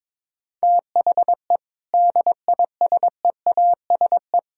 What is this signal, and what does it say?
Morse code, heard as a keyed tone, about 22 wpm, 700 Hz. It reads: THE DISEASE
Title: Sentences from Top 1000 Words - Review Part 6 - 22wpm